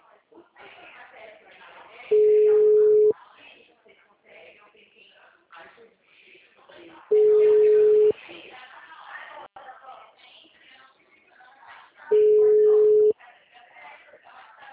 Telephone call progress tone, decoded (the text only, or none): ringback tone